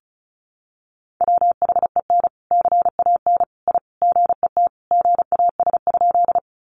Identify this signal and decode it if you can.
Morse code, heard as a keyed tone, about 34 wpm, 700 Hz. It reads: WHEN CAN I GET GAS?
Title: Sentences from Top 1100 Words 35wpm